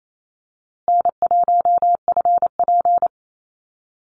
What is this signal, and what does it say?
Morse code, heard as a keyed tone, about 28 wpm, 700 Hz. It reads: N1FP
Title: USA Call Signs Repeated 3x 28wpm